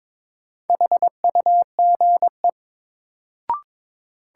Morse code, heard as a keyed tone, about 21 wpm, 700 Hz. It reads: HUGE
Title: Sentences from Top 1100 Words - New Word Drill Alt - 22wpm